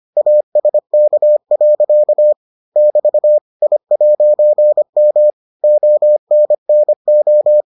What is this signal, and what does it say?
Morse code, heard as a keyed tone, about 25 wpm, 600 Hz. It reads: ASK. <BT> I'M ONNO